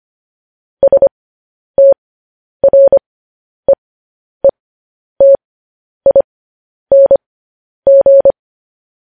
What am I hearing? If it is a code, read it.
Morse code, heard as a keyed tone, about 25 wpm, spaced as for 10 wpm, 550 Hz. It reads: STREETING
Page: news-headlines-in-morse-code-at-10-wpm